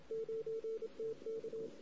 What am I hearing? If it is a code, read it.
Morse code, heard as a keyed tone, about 27 wpm, 440 Hz. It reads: 9TK
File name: CW / 2008-ss-cw-k5tr-op-n5rz -16:00